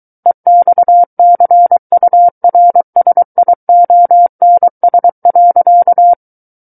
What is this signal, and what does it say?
Morse code, heard as a keyed tone, about 23 wpm, 700 Hz. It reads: EXCURSIONS.
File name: Book One / 14-Chapter 14 - 23 WPM